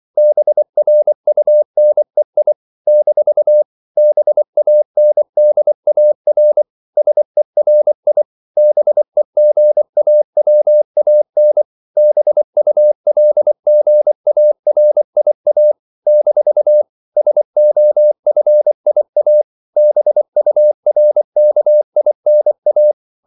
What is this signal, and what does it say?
Morse code, heard as a keyed tone, about 24 wpm, 600 Hz. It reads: BRUNEI - BANDAR SERI BEGAWAN BULGARIA - SOFIA BURKINA
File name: CW / Practice / NationalCapitals24